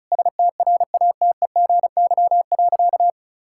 Morse code, heard as a keyed tone, about 36 wpm, 700 Hz. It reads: STRATEGY.
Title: Sentences from Top 1000 Words - Review Part 5 - 35wpm